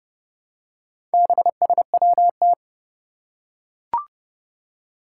Morse code, heard as a keyed tone, about 29 wpm, 700 Hz. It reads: BSWT
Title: Four Challenging Letter-Number Combinations 30wpm